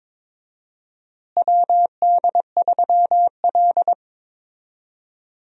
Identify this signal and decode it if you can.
Morse code, heard as a keyed tone, about 21 wpm, 700 Hz. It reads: WD3L
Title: USA Call Signs - 4 Characters - Repeated 2x - 22wpm